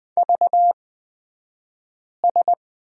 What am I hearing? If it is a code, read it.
Morse code, heard as a keyed tone, about 20 wpm, spaced as for 5 wpm, 700 Hz. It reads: VS
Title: 254 Lesson 41d - Period - Two Characters 20wpm